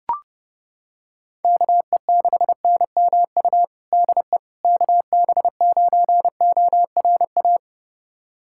Morse code, heard as a keyed tone, about 30 wpm, 700 Hz. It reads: KE6NMU DE KB9ORA
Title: USA Callsign DE Callsign 30wpm